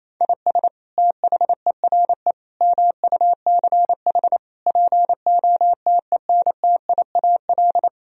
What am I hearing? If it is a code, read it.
Morse code, heard as a keyed tone, about 28 wpm, 700 Hz. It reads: IS THERE MUCH POTENTIAL
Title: Sentences from Top 1000 Words - Review Part 4 - 28wpm